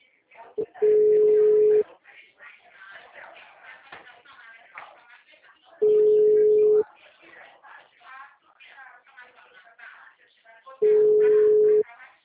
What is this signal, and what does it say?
Ringback tone (Europe, 425 Hz), with speech.
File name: out- -2024- - - .40477